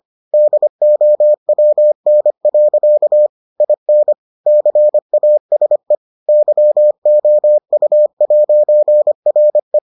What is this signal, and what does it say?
Morse code, heard as a keyed tone, about 25 wpm, 600 Hz. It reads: DOWN. IN CASE YOU'RE